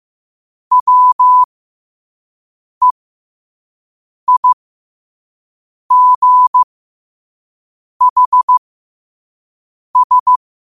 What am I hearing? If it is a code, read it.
Morse code, heard as a keyed tone, about 15 wpm, spaced as for 5 wpm, 1000 Hz. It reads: WEIGHS
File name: qotdfiles / 5wpm